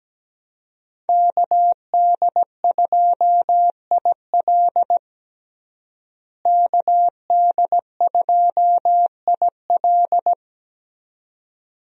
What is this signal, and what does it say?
Morse code, heard as a keyed tone, about 17 wpm, 700 Hz. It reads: KD2IL KD2IL
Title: USA Call Signs - 5 Characters - Repeated 2x - 17wpm